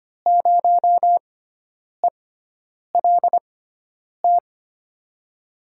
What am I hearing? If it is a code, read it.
Morse code, heard as a keyed tone, about 25 wpm, spaced as for 9 wpm, 700 Hz. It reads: 0ELT
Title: USA Call Signs - 6x Character Spacing - 25wpm